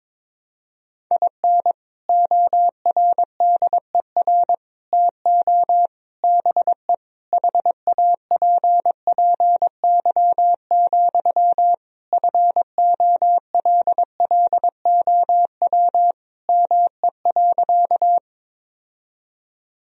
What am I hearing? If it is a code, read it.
Morse code, heard as a keyed tone, about 21 wpm, 700 Hz. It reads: IN ORDER TO BE HAPPY, FOLLOW ME.
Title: Sentences from Top 1000 Words - Review Part 2 - 22wpm